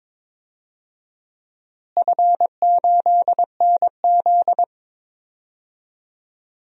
Morse code, heard as a keyed tone, about 22 wpm, 700 Hz. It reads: F8NZ
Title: DXCC-100 Call Signs - 4 Characters - Repeated 2x - 22wpm